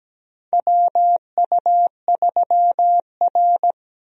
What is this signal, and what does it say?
Morse code, heard as a keyed tone, about 17 wpm, 700 Hz. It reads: WU3R
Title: USA Call Signs - 4 Characters - Repeated 2x - 17wpm